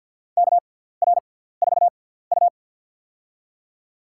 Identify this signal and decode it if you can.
Morse code, heard as a keyed tone, about 52 wpm, spaced as for 17 wpm, 700 Hz. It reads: KR4U
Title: USA Call Signs - 4 Characters - 6x Character Spacing - 50wpm